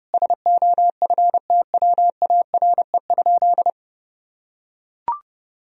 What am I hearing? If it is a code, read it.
Morse code, heard as a keyed tone, about 30 wpm, 700 Hz. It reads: SOFTWARE?
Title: Sentences from Top 1000 Words - Review Part 3 - 30wpm